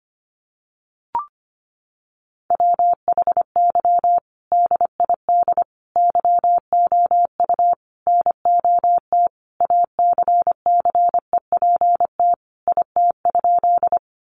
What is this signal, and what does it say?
Morse code, heard as a keyed tone, about 25 wpm, 700 Hz. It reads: WHY DID YOU NOT ACCEPT IT?
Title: Sentences from Top 1300 Words 25wpm